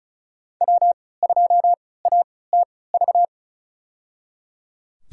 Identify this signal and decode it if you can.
Morse code, heard as a keyed tone, about 36 wpm, spaced as for 20 wpm, 690 Hz. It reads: W2ATV
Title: USA Call Signs - 3x Character Spacing - 35wpm